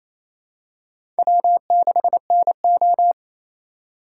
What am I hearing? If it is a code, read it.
Morse code, heard as a keyed tone, about 28 wpm, 700 Hz. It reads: W6NO
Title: USA Call Signs - 4 Characters - 3x Character Spacing - 28wpm